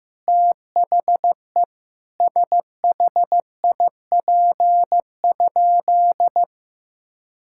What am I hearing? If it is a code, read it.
Morse code, heard as a keyed tone, about 15 wpm, 700 Hz. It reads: THE SHIP?
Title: Sentences from Top 1000 Words - Review Part 6 - 15wpm